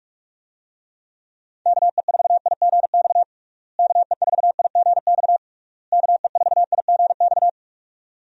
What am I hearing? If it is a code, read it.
Morse code, heard as a keyed tone, about 46 wpm, 700 Hz. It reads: KE4IGX KE4IGX KE4IGX
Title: USA Call Signs - 6 Characters - Repeated 3x - 45wpm